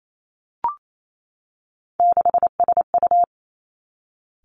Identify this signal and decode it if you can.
Morse code, heard as a keyed tone, about 28 wpm, 700 Hz. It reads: 6SU
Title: Three Challenging Letter-Number Combinations 28wpm